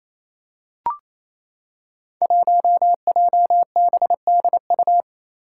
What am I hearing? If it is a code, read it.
Morse code, heard as a keyed tone, about 28 wpm, 700 Hz. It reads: 1JBDU
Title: Five Challenging Letter-Number Combinations 28wpm